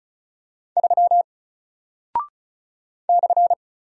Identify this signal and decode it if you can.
Morse code, heard as a keyed tone, about 36 wpm, 700 Hz. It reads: 3 /
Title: Single Character (Letter:Number:Punctuation:Prosign) Alt 35wpm